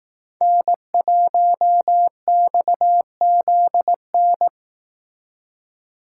Morse code, heard as a keyed tone, about 18 wpm, 700 Hz. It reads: N1XZN
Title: Speed Racing - USA 1x3 Call Signs 15wpm